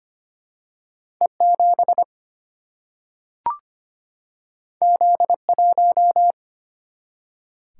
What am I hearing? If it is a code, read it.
Morse code, heard as a keyed tone, about 25 wpm, 700 Hz. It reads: E7 Z1
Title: Single-Letter Single-Number 25wpm